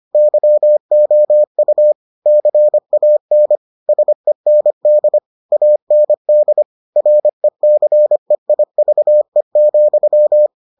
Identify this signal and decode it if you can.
Morse code, heard as a keyed tone, about 25 wpm, 600 Hz. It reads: YOU CAN SEND AND RECEIVE,